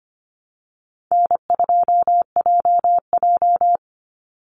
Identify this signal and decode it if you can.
Morse code, heard as a keyed tone, about 25 wpm, 700 Hz. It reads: N2JJ
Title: Easy Call Signs Repeated 2x - 25wpm